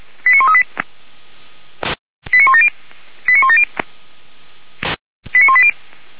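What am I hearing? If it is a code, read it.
Five-tone selective call (ZVEI), heard as quick runs of five tones, tones of about 70 ms. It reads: 89138 89170 89170 89180
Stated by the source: Five-tone ZVEI selective call received by SDR, from the audio server of Feuerwehr Orlamünde (fire brigade).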